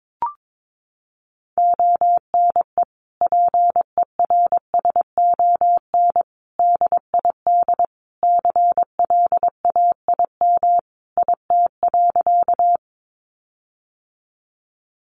Morse code, heard as a keyed tone, about 22 wpm, 700 Hz. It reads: ONE PERSON DID CLAIM IT.
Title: Sentences from Top 1500 Words 22wpm